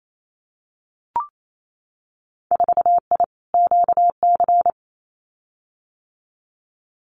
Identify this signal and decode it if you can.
Morse code, heard as a keyed tone, about 28 wpm, 700 Hz. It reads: 4I QC
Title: Winter Field Day Exchanges Encore 28wpm